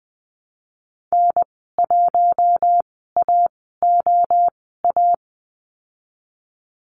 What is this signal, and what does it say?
Morse code, heard as a keyed tone, about 20 wpm, spaced as for 15 wpm, 700 Hz. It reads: N1AOA
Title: 048 Lesson 9g - Number 4 - Call Signs 20wpm